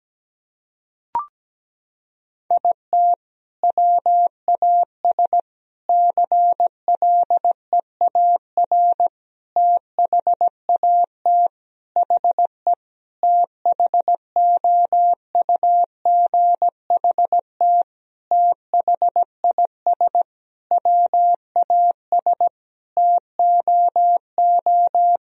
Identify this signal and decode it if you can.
Morse code, heard as a keyed tone, about 17 wpm, 700 Hz. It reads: IT WAS CLEAR THAT HE THOUGHT THIS WAS TOO
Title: Sentences from Top 1000 Words - Review Part 4 - 17wpm